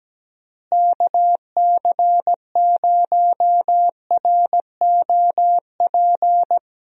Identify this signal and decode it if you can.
Morse code, heard as a keyed tone, about 17 wpm, 700 Hz. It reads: KC0ROP
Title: USA Call Signs - 6 Characters - Repeated 2x - 17wpm